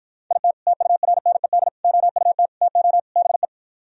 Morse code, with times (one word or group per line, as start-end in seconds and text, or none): 0.3-0.5 IT
0.7-1.7 TURNED
1.8-2.5 OUT
2.6-3.0 TO
3.2-3.5 BE